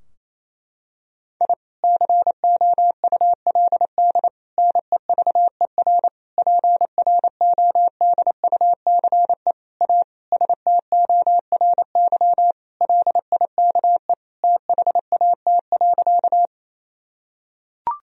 Morse code, heard as a keyed tone, about 28 wpm, 700 Hz. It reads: I COULD NEVER PRODUCE A STORY LIKE THAT.
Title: Sentences from Top 1000 Words - Review Part 3 - 28wpm